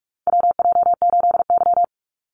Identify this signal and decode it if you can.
Morse code, heard as a keyed tone, about 44 wpm, 700 Hz. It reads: WJ8Y
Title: Call Signs Repeated 2x - 45wpm